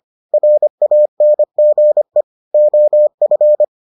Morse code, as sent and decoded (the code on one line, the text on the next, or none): .-. .- -. --. / --- ..-.
RANGE OF